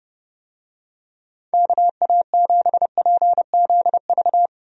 Morse code, heard as a keyed tone, about 32 wpm, 700 Hz. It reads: KA7PZV
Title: USA Call Signs - 6 Characters - 30wpm